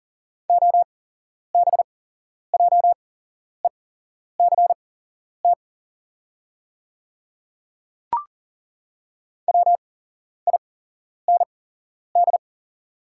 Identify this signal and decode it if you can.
Morse code, heard as a keyed tone, about 40 wpm, spaced as for 11 wpm, 690 Hz. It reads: OBJECT WIND
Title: 8x Character Spacing - Top 500 Words - 40wpm